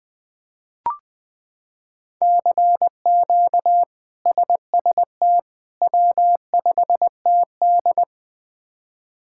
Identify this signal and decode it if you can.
Morse code, heard as a keyed tone, about 20 wpm, 700 Hz. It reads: CQ SST W5TD
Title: K1USN SST QSO Practice - 20wpm